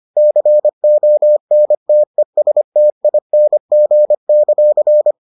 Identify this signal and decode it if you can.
Morse code, heard as a keyed tone, about 25 wpm, 600 Hz. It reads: CONTESTING;